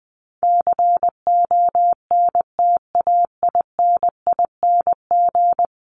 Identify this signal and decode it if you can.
Morse code, heard as a keyed tone, about 20 wpm, 700 Hz. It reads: CONTAINING
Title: ING Suffix - Top 100 Words with Base Words 20wpm